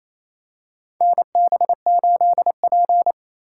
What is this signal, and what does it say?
Morse code, heard as a keyed tone, about 28 wpm, 700 Hz. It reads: NB8P
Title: USA Call Signs - 4 Characters - 2x Character Spacing - 28wpm